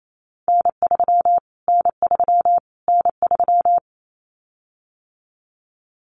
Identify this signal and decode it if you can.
Morse code, heard as a keyed tone, about 28 wpm, 700 Hz. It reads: N3 N3 N3
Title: Callsign Prefixes for Top 25 DXCC - Repeated 3x - 28wpm